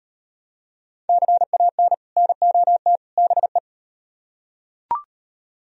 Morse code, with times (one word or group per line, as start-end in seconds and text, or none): 1.1-1.9 CAN
2.2-3.0 NOT
3.2-3.6 BE